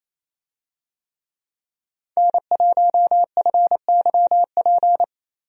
Morse code, heard as a keyed tone, about 28 wpm, 700 Hz. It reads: N1FYP
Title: USA Call Signs - 5 Characters - Repeated 2x - 28wpm